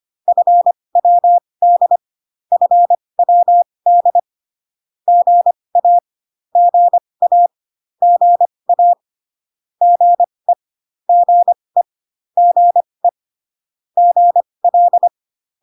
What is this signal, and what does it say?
Morse code, heard as a keyed tone, about 25 wpm, spaced as for 20 wpm, 700 Hz. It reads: FWD FWD GA GA GA GE GE GE GL